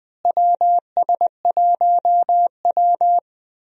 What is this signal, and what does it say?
Morse code, heard as a keyed tone, about 20 wpm, 700 Hz. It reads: WS1W
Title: USA Call Signs - 4 Characters - Repeated 2x - 20wpm